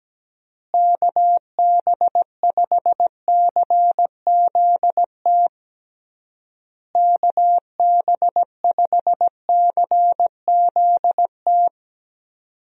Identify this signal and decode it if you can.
Morse code, heard as a keyed tone, about 17 wpm, 700 Hz. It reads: KB5CZT KB5CZT
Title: USA Call Signs - 6 Characters - Repeated 3x - 17wpm